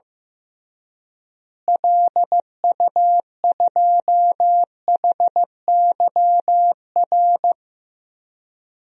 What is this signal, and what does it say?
Morse code, heard as a keyed tone, about 15 wpm, 700 Hz. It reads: LU2HYR
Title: DXCC-100 Call Signs - 6 Characters - Repeated 3x - 15wpm